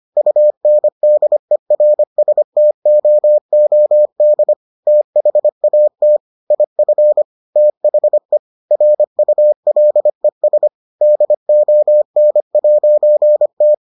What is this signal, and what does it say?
Morse code, heard as a keyed tone, about 25 wpm, 600 Hz. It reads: UNDERSTOOD THAT IF THE RULES DON'T